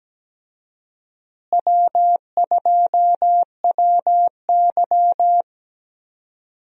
Morse code, heard as a keyed tone, about 17 wpm, 700 Hz. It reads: W2WY
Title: USA Call Signs - 4 Characters - Repeated 3x - 17wpm